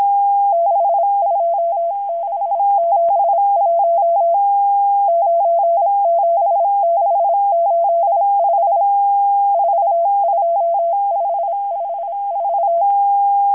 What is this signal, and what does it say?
Morse code, heard as a keyed tone, about 26 wpm, 680 Hz. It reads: 62671 97685 42554
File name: HM02- -0437utc-7351khz